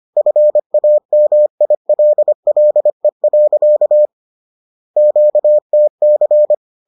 Morse code, heard as a keyed tone, about 25 wpm, 600 Hz. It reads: FAMILLE. QTC
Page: bulletin UFT sous forme texte et en CW